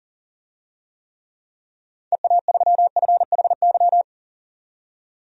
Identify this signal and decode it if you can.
Morse code, heard as a keyed tone, about 40 wpm, 700 Hz. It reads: EA3FHY
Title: DXCC-100 Call Signs - 6 Characters - 40wpm